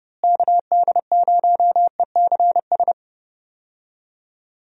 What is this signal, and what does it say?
Morse code, heard as a keyed tone, about 30 wpm, 700 Hz. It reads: KD0ECS
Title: USA Call Signs - 6 Characters - Repeated 3x - 30wpm